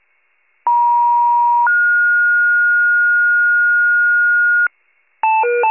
Two-tone page, heard as a one-second tone, then a three-second tone, about 950 then 1430 Hz, with noise. It is heard from 0.7 to 4.7 s.
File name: wcalert